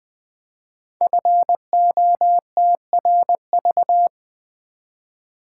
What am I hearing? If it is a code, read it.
Morse code, heard as a keyed tone, about 20 wpm, 700 Hz. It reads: FOTRV
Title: Code Groups - 5 Letters - 20wpm